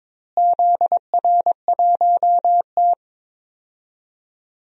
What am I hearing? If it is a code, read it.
Morse code, heard as a keyed tone, about 22 wpm, 700 Hz. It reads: ZR1T
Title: DXCC-100 Call Signs - 4 Characters - Repeated 3x - 22wpm